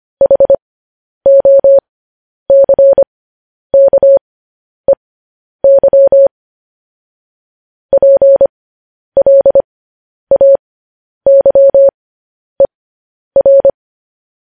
Morse code, heard as a keyed tone, about 25 wpm, spaced as for 10 wpm, 550 Hz. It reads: HOCKEY PLAYER
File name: podcast / News headlines-1x-25x10